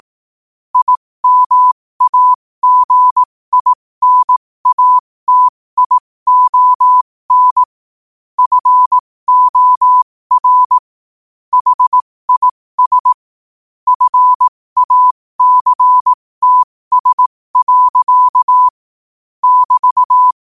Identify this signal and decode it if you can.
Morse code, heard as a keyed tone, about 18 wpm, spaced as for 15 wpm, 1000 Hz. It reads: IMAGINATION FOR HIS FACTS. <BT>
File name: qotdfiles / 15wpm